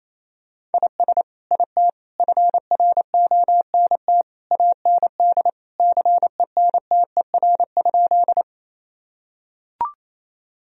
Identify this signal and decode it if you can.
Morse code, heard as a keyed tone, about 28 wpm, 700 Hz. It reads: IS IT FRONT AND CENTER?
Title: Sentences from Top 1000 Words - Review Part 2 - 28wpm